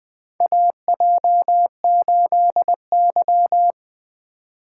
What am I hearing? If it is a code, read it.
Morse code, heard as a keyed tone, about 20 wpm, 700 Hz. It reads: AJ8Y
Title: USA Call Signs - 2x Character Spacing - 20wpm